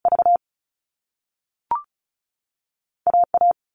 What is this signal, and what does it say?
Morse code, heard as a keyed tone, about 36 wpm, spaced as for 3 wpm, 690 Hz. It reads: VA A
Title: All Cut Numbers - 2-Digit 35wpm